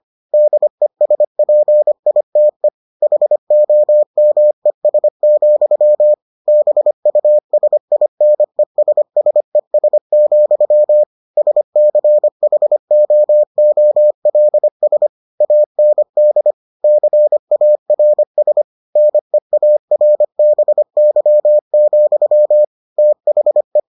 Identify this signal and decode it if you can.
Morse code, heard as a keyed tone, about 25 wpm, 600 Hz. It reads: DESPITE HOMES, BUSINESSES, SCHOOLS AND CARS NEARBY, THE